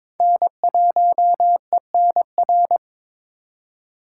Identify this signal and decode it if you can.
Morse code, heard as a keyed tone, about 26 wpm, 700 Hz. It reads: N1ENR
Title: Speed Racing - USA 1x3 Call Signs 22wpm